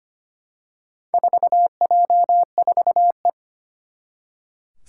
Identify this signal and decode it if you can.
Morse code, heard as a keyed tone, about 25 wpm, 700 Hz. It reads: 4J4E
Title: Four Challenging Letter-Number Combinations 25wpm